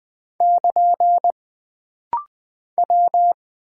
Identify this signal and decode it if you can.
Morse code, heard as a keyed tone, about 20 wpm, 700 Hz. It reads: <KN> W
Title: Single Character (Letter:Number:Punctuation:Prosign) - Rapid-Fire - 20wpm